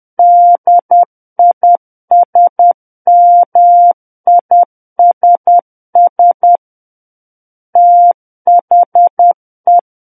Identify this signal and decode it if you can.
Morse code, heard as a keyed tone, about 10 wpm, 700 Hz. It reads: DISMISS THE
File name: Book One / 01-Chapter 1 - 10 WPM